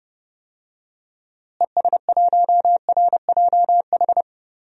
Morse code, heard as a keyed tone, about 30 wpm, 700 Hz. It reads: ES1RJH
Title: DXCC-100 Call Signs - 6 Characters - Repeated 2x - 30wpm